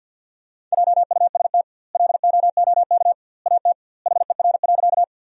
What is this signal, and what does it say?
Morse code, heard as a keyed tone, about 53 wpm, 700 Hz. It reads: JUST LOOK AT HER.
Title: 2x Word-Spacing - Sentences from Top 200 Words - 50wpm